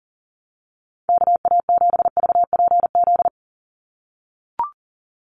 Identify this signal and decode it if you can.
Morse code, heard as a keyed tone, about 40 wpm, 700 Hz. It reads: KA7VPZ
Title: USA Call Signs 40wpm